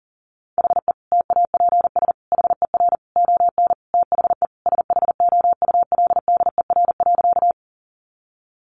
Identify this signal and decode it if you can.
Morse code, heard as a keyed tone, about 40 wpm, 700 Hz. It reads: HE TAPS HER ON THE SHOULDER.
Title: Additional Easy Phrases 40wpm